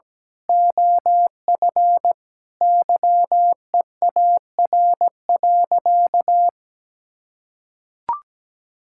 Morse code, heard as a keyed tone, about 17 wpm, 700 Hz. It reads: OF YEAR.
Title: Sentences from Top 1000 Words - Review Part 6 - 17wpm